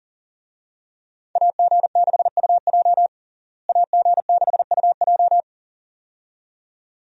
Morse code, heard as a keyed tone, about 40 wpm, 700 Hz. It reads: AG6UJ AG6UJ
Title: USA Call Signs - 5 Characters - Repeated 2x - 40wpm